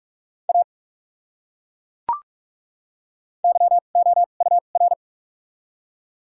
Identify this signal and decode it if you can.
Morse code, heard as a keyed tone, about 41 wpm, spaced as for 31 wpm, 700 Hz. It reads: A YOUR
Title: 2x Character Spacing - Top 100 Words - 45wpm